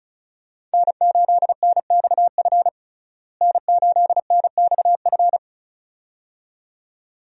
Morse code, heard as a keyed tone, about 34 wpm, 700 Hz. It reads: N8NXF N8NXF
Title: USA Call Signs - 5 Characters - Repeated 2x - 35wpm